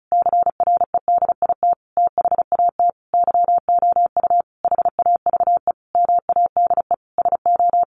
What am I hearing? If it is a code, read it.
Morse code, heard as a keyed tone, about 34 wpm, 700 Hz. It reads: CREDIT THAT YOU HAVE MADE SO
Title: Sentences from Top 1400 Words 35wpm